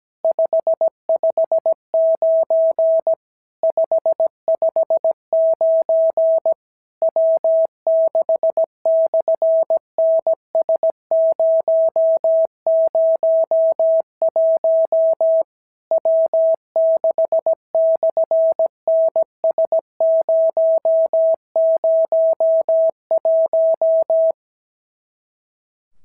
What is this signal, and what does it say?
Morse code, heard as a keyed tone, about 17 wpm, 650 Hz. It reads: 559 559 W6/NS001 W6/NS001
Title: SOTA QSO Practice 17wpm